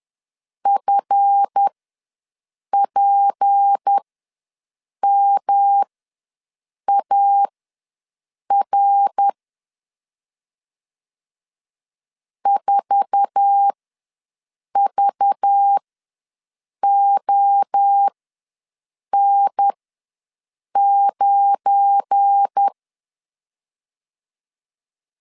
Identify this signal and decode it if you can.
Morse code, heard as a keyed tone, about 10 wpm, spaced as for 6 wpm, 800 Hz. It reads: FPMAR 4VON9
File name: LEZIONI / 9 28 ws 10wpm 4124